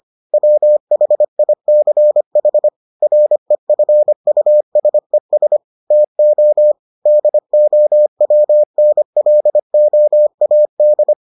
Morse code, heard as a keyed tone, about 25 wpm, 600 Hz. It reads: WHICH REFUSES TO DOWNLOAD